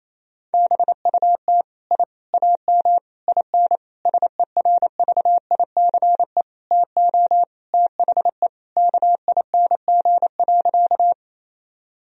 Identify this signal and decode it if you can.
Morse code, heard as a keyed tone, about 28 wpm, 700 Hz. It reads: BUT I AM IN SERVICE TO THE KING.
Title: Sentences from Top 1000 Words - Review Part 4 - 28wpm